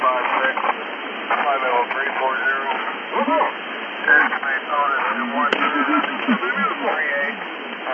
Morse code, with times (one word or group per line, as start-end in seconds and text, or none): none